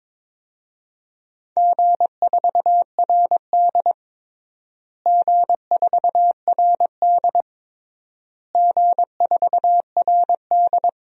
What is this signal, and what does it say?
Morse code, heard as a keyed tone, about 22 wpm, 700 Hz. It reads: G4RD G4RD G4RD
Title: DXCC-100 Call Signs - 4 Characters - Repeated 3x - 22wpm